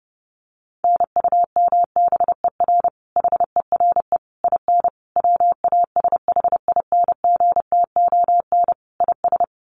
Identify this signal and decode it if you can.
Morse code, heard as a keyed tone, about 30 wpm, 700 Hz. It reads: NUMBER HERE IN WASHINGTON IS
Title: Sets of 5 Words Encore2 30wpm